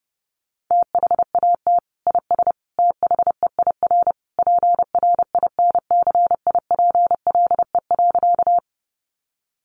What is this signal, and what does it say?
Morse code, heard as a keyed tone, about 30 wpm, 700 Hz. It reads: THAT IS THEIR PRINCIPLE.
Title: Sentences from Top 1400 Words 30wpm